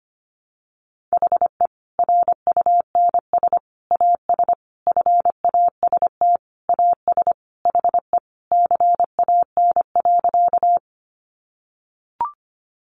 Morse code, heard as a keyed tone, about 25 wpm, 700 Hz. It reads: HE RUNS AS FAST AS HE CAN.
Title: Additional Easy Phrases 25wpm